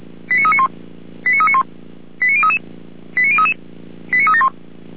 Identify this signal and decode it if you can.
Five-tone selective call (ZVEI), heard as quick runs of five tones, tones of about 70 ms. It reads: 89391 89391 89033 89033 89371